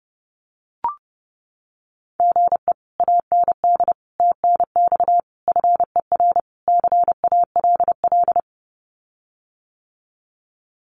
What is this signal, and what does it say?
Morse code, heard as a keyed tone, about 30 wpm, 700 Hz. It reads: GE AND TNX FER CALL
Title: Common QSO Phrases Encore 30wpm